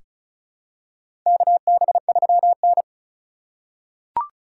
Morse code, heard as a keyed tone, about 34 wpm, 700 Hz. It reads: KB3N
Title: USA Call Signs - 4 Characters - 3x Character Spacing - 35wpm